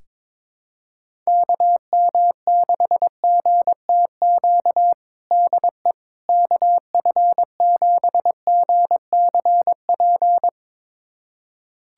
Morse code, heard as a keyed tone, about 22 wpm, 700 Hz. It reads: KM6GTQ DE KF7GCP
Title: USA Callsign DE Callsign - 6 Characters - 22wpm